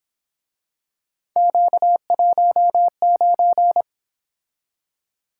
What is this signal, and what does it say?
Morse code, heard as a keyed tone, about 26 wpm, 700 Hz. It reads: Q19
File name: Speed-Racing - Three Letter-Number Combinations - 22wpm